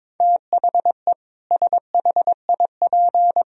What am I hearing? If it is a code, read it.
Morse code, heard as a keyed tone, about 22 wpm, 700 Hz. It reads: THE SHIP
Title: Sentences from Top 1000 Words - Review Part 4 - 22wpm